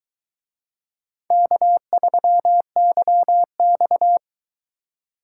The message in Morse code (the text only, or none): K3YX